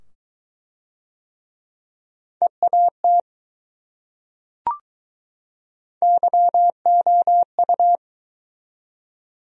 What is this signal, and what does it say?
Morse code, heard as a keyed tone, about 20 wpm, 700 Hz. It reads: EAT YOU